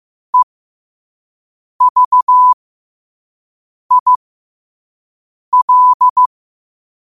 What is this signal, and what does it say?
Morse code, heard as a keyed tone, about 15 wpm, spaced as for 5 wpm, 1000 Hz. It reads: EVIL
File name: qotdfiles / 5wpm